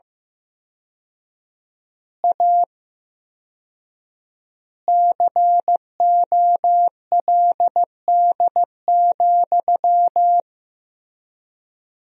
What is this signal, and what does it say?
Morse code, heard as a keyed tone, about 15 wpm, 700 Hz. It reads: A COLD,
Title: 4x Word-Spacing - Sentences from Top 500 Words - 15wpm